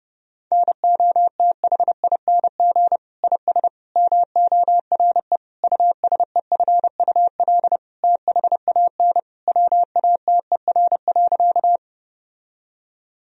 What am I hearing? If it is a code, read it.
Morse code, heard as a keyed tone, about 30 wpm, 700 Hz. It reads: NOTHING IS MORE USEFUL THAN WATER.
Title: Sentences from Top 1000 Words - Review Part 4 - 30wpm